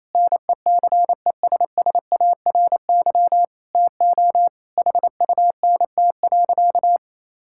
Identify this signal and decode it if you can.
Morse code, heard as a keyed tone, about 28 wpm, 700 Hz. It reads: NECESSARY TO HUNT.
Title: Sentences from Top 1600 Words 28wpm